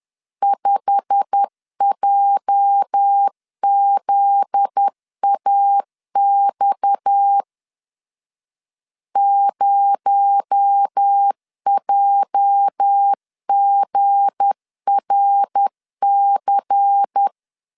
Morse code, heard as a keyed tone, about 10 wpm, 800 Hz. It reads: 5JZAX 0JGRC
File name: LEZIONI / 8 YJ45 / 8 YJ45 10wpm 2549